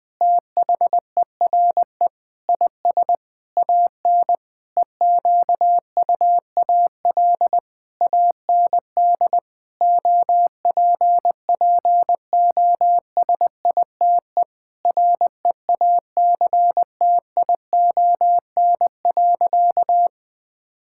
Morse code, heard as a keyed tone, about 20 wpm, 700 Hz. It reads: THERE IS AN EQUAL AND OPPOSITE REACTION.
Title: Sentences with 8-Word Phrases - Spoken Beginning - 20wpm